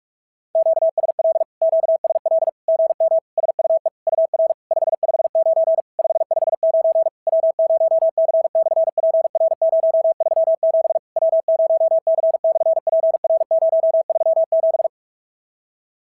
Morse code, heard as a keyed tone, about 38 wpm, 660 Hz. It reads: QSL QSL GM SUE UR 559 559 W0C/PR037 W0C/PR037
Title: SOTA QSO Practice 45wpm